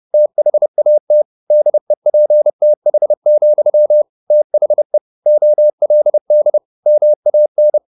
Morse code, heard as a keyed tone, about 30 wpm, 600 Hz. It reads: THAT DEPTH, THE OLD MAN
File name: the old man and the sea / the old man and the sea 1 30wpm